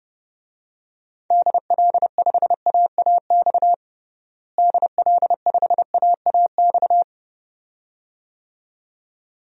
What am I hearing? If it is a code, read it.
Morse code, heard as a keyed tone, about 31 wpm, 700 Hz. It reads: DL5AAX DL5AAX
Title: DXCC-100 Call Signs - 6 Characters - Repeated 2x - 30wpm